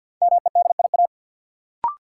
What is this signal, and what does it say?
Morse code, heard as a keyed tone, about 48 wpm, 700 Hz. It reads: MEDIA
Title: Sentences from Top 1000 Words - Review Part 4 - New Word Drill - 50wpm